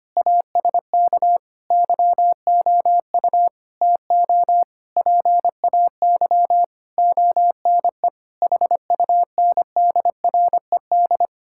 Morse code, heard as a keyed tone, about 25 wpm, 700 Hz. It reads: ASK YOU TO PAY ONE HUNDRED
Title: Sentences from Top 1000 Words - Review Part 6 - 25wpm